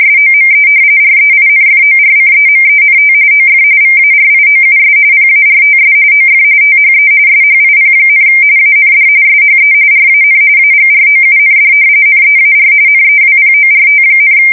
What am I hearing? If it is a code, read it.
RTTY, about 45.45 baud. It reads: SOME REINFORCEMENT OF VARIOUS PARTS OF THE SHIP, AND SHE RAN AGROUND NEAR NEWPORT,
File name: fsk170 / battleships